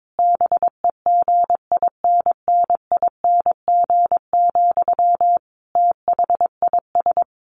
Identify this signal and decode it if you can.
Morse code, heard as a keyed tone, about 22 wpm, 700 Hz. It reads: BEGINNING, THIS
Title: Sentences from Top 1400 Words 22wpm